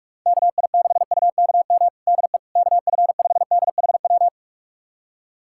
Morse code, heard as a keyed tone, about 44 wpm, 700 Hz. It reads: KI6UKM DE KF5DHW
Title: USA Callsign DE Callsign - 6 Characters - 45wpm